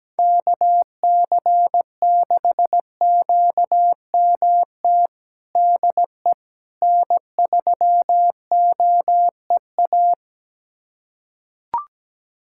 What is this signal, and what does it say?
Morse code, heard as a keyed tone, about 17 wpm, 700 Hz. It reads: KC6QMT DE N3OEA
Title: USA Callsign DE Callsign 17wpm